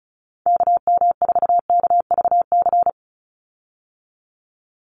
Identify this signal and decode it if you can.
Morse code, heard as a keyed tone, about 34 wpm, 700 Hz. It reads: KM4KVC
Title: USA Call Signs 35wpm